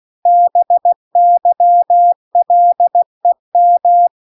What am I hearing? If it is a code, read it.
Morse code, heard as a keyed tone, about 16 wpm, 700 Hz. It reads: BYLEM